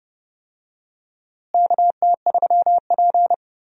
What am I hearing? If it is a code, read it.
Morse code, heard as a keyed tone, about 30 wpm, 700 Hz. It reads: KT3P
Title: USA Call Signs - 4 Characters - Repeated 2x - 30wpm